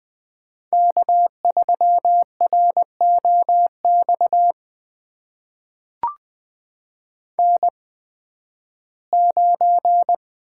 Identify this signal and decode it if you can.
Morse code, heard as a keyed tone, about 20 wpm, spaced as for 13 wpm, 700 Hz. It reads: K3ROX N9
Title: USA Call Signs - 5 Characters - 8x Character Spacing - 20wpm